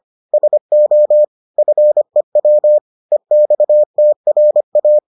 Morse code, heard as a keyed tone, about 25 wpm, 600 Hz. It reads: SO FEW EXTRA